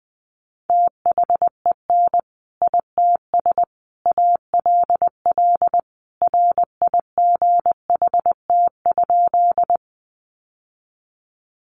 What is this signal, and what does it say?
Morse code, heard as a keyed tone, about 20 wpm, 700 Hz. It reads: THEN ITS ALL RIGHT?
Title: Sentences from Top 600 Words 20wpm